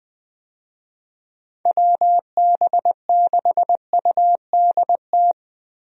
Morse code, heard as a keyed tone, about 20 wpm, 700 Hz. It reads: WB6UDT
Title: USA Call Signs - 6 Characters - Repeated 3x - 20wpm